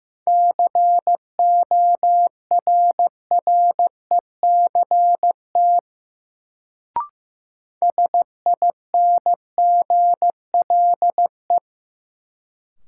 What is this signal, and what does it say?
Morse code, heard as a keyed tone, about 15 wpm, 700 Hz. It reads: CORRECT SINGLE
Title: Sentences from Top 700 Words - New Word Drill - 15wpm